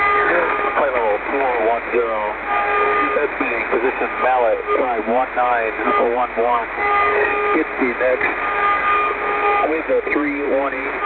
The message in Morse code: none